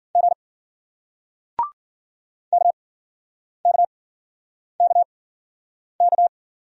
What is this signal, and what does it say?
Morse code, heard as a keyed tone, about 54 wpm, spaced as for 9 wpm, 700 Hz. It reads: G KKKK